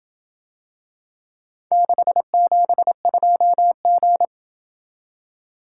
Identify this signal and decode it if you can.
Morse code, heard as a keyed tone, about 27 wpm, 700 Hz. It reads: 672G